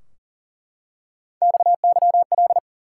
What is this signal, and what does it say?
Morse code, heard as a keyed tone, about 40 wpm, 700 Hz. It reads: XYL
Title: Top 90 QSO Elements 40wpm